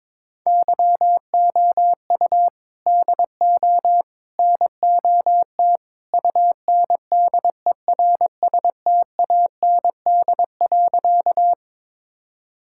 Morse code, heard as a keyed tone, about 22 wpm, 700 Hz. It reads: YOU DO NOT UNDERSTAND.
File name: practice / 22wpm / Sentences with 3-Word Phrases - 22wpm